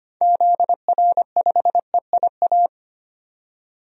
Morse code, heard as a keyed tone, about 25 wpm, 700 Hz. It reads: ZR5EIA
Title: DXCC-100 Call Signs - 6 Characters - 25wpm